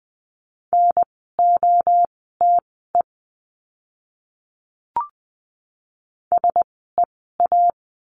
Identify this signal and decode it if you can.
Morse code, heard as a keyed tone, about 20 wpm, spaced as for 14 wpm, 700 Hz. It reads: NOTE SEA
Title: 054 Lesson 10f - Letter r - Words 20wpm